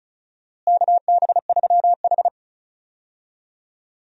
Morse code, heard as a keyed tone, about 34 wpm, 700 Hz. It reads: KB3H
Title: USA Call Signs - 4 Characters - Repeated 3x - 35wpm